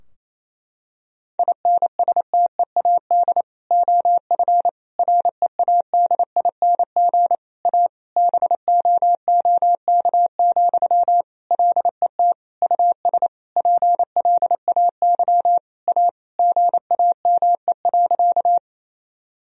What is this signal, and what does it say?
Morse code, heard as a keyed tone, about 28 wpm, 700 Hz. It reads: INSTEAD OF READING A BOOK, LET US PLAY A GAME.
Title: Sentences from Top 1000 Words - Review Part 4 - 28wpm